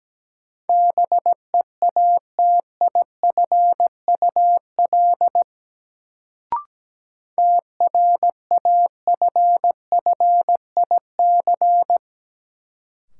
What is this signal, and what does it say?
Morse code, heard as a keyed tone, about 17 wpm, 700 Hz. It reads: BEATIFUL TRAFFIC
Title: Sentences from Top 1300 Words - New Word Drill Alt - 17wpm